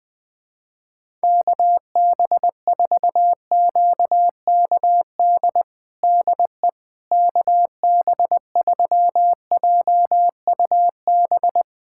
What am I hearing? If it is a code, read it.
Morse code, heard as a keyed tone, about 19 wpm, 700 Hz. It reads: KB4QKD DE KB3JUB
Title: USA Callsign DE Callsign - 6 Characters - 20wpm